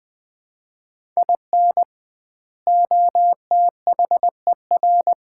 Morse code, heard as a keyed tone, about 21 wpm, 700 Hz. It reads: IN OTHER
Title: 2x Word-Spacing - Sets of 3 Words - 20wpm